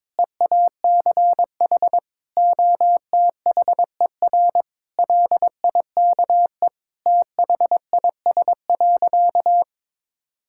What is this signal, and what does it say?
Morse code, heard as a keyed tone, about 22 wpm, 700 Hz. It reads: EACH OTHER LIKE THIS.
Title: Sentences from Top 1000 Words - Review Part 6 - 22wpm